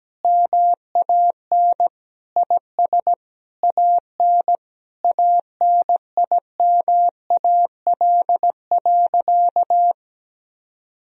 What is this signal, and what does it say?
Morse code, heard as a keyed tone, about 17 wpm, 700 Hz. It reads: MAN IS AN ANIMAL.
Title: Sentences from Top 1000 Words - Review Part 1 - 17wpm